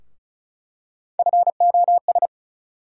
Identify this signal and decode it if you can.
Morse code, heard as a keyed tone, about 36 wpm, 700 Hz. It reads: FOS
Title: USA Call Sign Postfixes - 3 Letters 35wpm